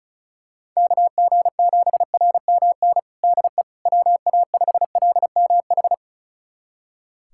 Morse code, heard as a keyed tone, about 36 wpm, 700 Hz. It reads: KG7RMN DE WA5LMH
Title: USA Callsign DE Callsign - 6 Characters - 35wpm